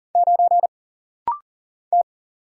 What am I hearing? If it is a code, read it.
Morse code, heard as a keyed tone, about 40 wpm, 700 Hz. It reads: 9 T
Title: Single Letter-Number - Rapid-Fire 40wpm